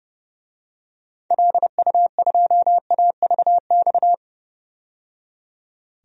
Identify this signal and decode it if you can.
Morse code, heard as a keyed tone, about 31 wpm, 700 Hz. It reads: LU2AVX
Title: DXCC-100 Call Signs - 6 Characters - 30wpm